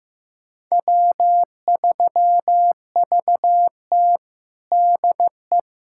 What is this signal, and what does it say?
Morse code, heard as a keyed tone, about 15 wpm, 700 Hz. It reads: W3VT DE
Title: USA Callsign DE Callsign - 4 Characters - 15wpm